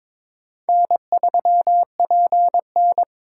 Morse code, heard as a keyed tone, about 22 wpm, 700 Hz. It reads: N3PN
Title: USA Call Signs - 4 Characters - 4x Character Spacing - 22wpm